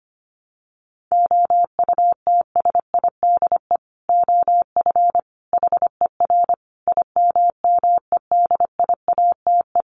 Morse code, heard as a keyed tone, about 25 wpm, 700 Hz. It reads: OUTSIDE OF HER IMMEDIATE
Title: Sentences from Top 1200 Words 25wpm